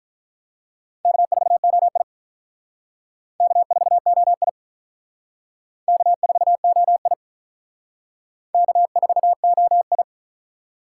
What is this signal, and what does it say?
Morse code, heard as a keyed tone, about 44 wpm, 700 Hz. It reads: K4OI K4OI K4OI K4OI